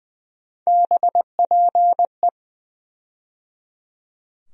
Morse code, heard as a keyed tone, about 20 wpm, 700 Hz. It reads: BPE
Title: USA Call Sign Postfixes - 3 Letters 20wpm